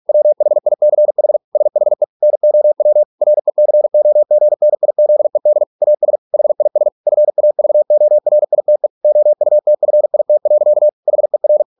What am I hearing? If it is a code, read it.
Morse code, heard as a keyed tone, about 46 wpm, 600 Hz. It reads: WHICH SHE NOW RECOGNIZED AS HIS FAVORITE OUTFIT. HER